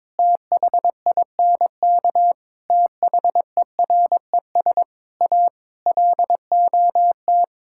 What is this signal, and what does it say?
Morse code, heard as a keyed tone, about 22 wpm, 700 Hz. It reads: THINK THERES A LOT
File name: practice / 22wpm / Sets of 6 Words - 22wpm